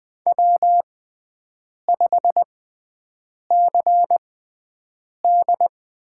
Morse code, heard as a keyed tone, about 20 wpm, spaced as for 7 wpm, 700 Hz. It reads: W5CD
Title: USA Call Signs - 5 Characters - 6x Character Spacing - 20wpm